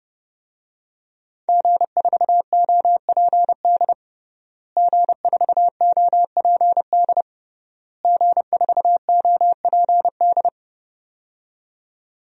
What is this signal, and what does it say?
Morse code, heard as a keyed tone, about 30 wpm, 700 Hz. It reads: G4OPD G4OPD G4OPD
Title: DXCC-100 Call Signs - 5 Characters - Repeated 3x - 30wpm